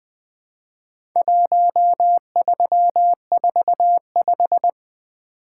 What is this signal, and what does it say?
Morse code, heard as a keyed tone, about 20 wpm, 700 Hz. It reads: 1345
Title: Speed-Racing - Four Letter-Number Combinations - 20wpm